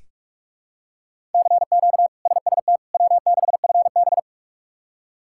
Morse code, heard as a keyed tone, about 43 wpm, 700 Hz. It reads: CQ SST W6FB
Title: K1USN SST QSO Practice - 45wpm